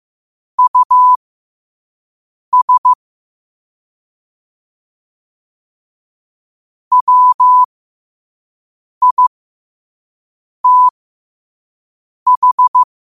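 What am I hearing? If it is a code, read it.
Morse code, heard as a keyed tone, about 15 wpm, spaced as for 5 wpm, 1000 Hz. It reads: US WITH
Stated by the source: Morse code at 5 wpm.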